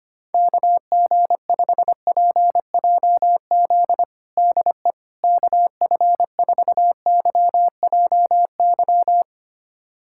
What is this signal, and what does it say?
Morse code, heard as a keyed tone, about 25 wpm, 700 Hz. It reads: KG5PJZ DE KF4YJY
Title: USA Callsign DE Callsign - 6 Characters - 25wpm